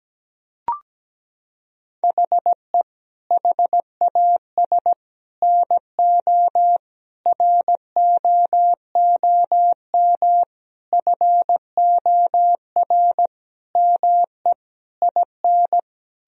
Morse code, heard as a keyed tone, about 17 wpm, 700 Hz. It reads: HE HAS NO ROOM FOR ME IN
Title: Sentences from Top 1000 Words - Review Part 4 - 17wpm